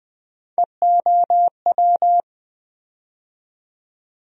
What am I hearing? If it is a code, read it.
Morse code, heard as a keyed tone, about 21 wpm, 700 Hz. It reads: EOW
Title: USA Call Sign Postfixes - 3 Letters 20wpm